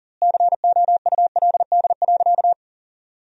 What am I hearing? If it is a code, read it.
Morse code, heard as a keyed tone, about 37 wpm, 700 Hz. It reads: COULD.
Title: 4x Word-Spacing - Sentences from Top 500 Words - 40wpm